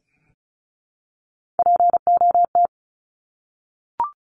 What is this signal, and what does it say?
Morse code, heard as a keyed tone, about 34 wpm, 700 Hz. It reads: POT
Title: Top 751-1000 Words 35wpm